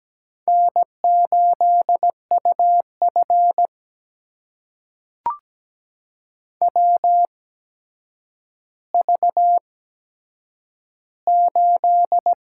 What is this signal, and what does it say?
Morse code, heard as a keyed tone, about 17 wpm, spaced as for 8 wpm, 700 Hz. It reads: N8UF WV8
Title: USA Call Signs - 4 Characters - 8x Character Spacing - 17wpm